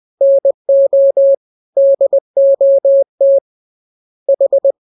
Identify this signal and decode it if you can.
Morse code, heard as a keyed tone, about 20 wpm, 550 Hz. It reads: NO DOT H
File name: jokes / joke 20 20 0000